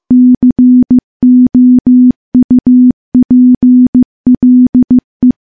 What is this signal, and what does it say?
Morse code, heard as a keyed tone, about 15 wpm, 250 Hz. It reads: COUPLE
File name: morse-training-15WPM